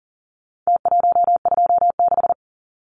Morse code, heard as a keyed tone, about 40 wpm, 700 Hz. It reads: T126
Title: Numbers - Contest Style - 4-Digit 40wpm